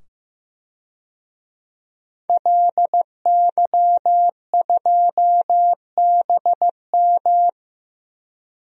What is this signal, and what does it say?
Morse code, heard as a keyed tone, about 15 wpm, 700 Hz. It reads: LY2BM
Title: DXCC-100 Call Signs - 5 Characters - 15wpm